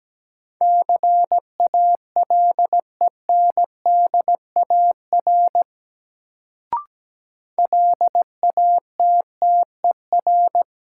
Morse code, heard as a keyed tone, about 17 wpm, 700 Hz. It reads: CALENDAR LATTER
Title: Sentences from Top 1600 Words - New Word Drill - 17wpm